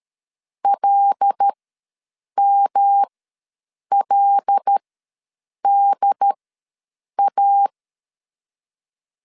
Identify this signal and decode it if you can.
Morse code, heard as a keyed tone, about 13 wpm, spaced as for 7 wpm, 800 Hz. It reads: LMLDA
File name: LEZIONI / 5 LBVC / 5 L ws 12wpm 2958